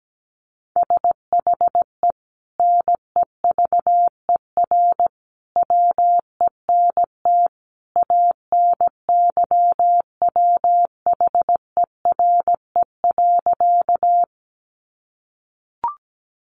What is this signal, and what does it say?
Morse code, heard as a keyed tone, about 17 wpm, 700 Hz. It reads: SHE NEVER WENT ANYWHERE.
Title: Sentences from Top 1400 Words 17wpm